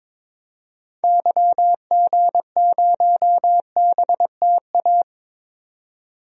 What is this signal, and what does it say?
Morse code, heard as a keyed tone, about 21 wpm, 700 Hz. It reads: YG0BTA
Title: DXCC-100 Call Signs - 6 Characters - 22wpm